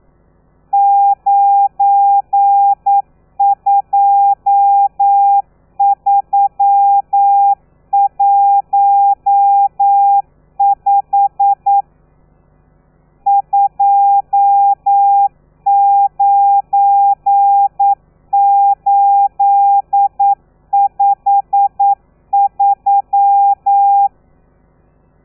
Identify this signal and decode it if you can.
Morse code, heard as a keyed tone, about 9 wpm, 800 Hz. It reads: 92315 29853
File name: telegrafia / i2tfj / Lez12 40